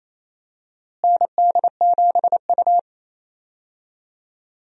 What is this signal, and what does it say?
Morse code, heard as a keyed tone, about 27 wpm, 700 Hz. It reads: ND7U
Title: USA Call Signs - 4 Characters - Repeated 3x - 28wpm